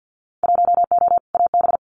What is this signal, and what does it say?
Morse code, heard as a keyed tone, about 50 wpm, 700 Hz. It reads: 1O AB
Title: Winter Field Day Exchanges Encore 50wpm